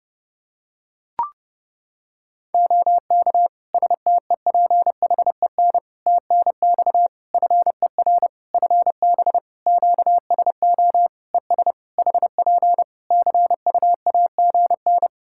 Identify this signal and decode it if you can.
Morse code, heard as a keyed tone, about 30 wpm, 700 Hz. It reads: OK STEPHEN TNX FER FB QSO ES HP CUAGN
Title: K1USN SST QSO Practice 30wpm